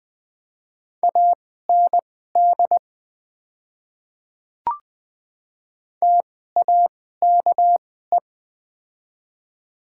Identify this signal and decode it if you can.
Morse code, heard as a keyed tone, about 20 wpm, spaced as for 14 wpm, 700 Hz. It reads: AND TAKE
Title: 2x Character Spacing - Top 300 Words - 20wpm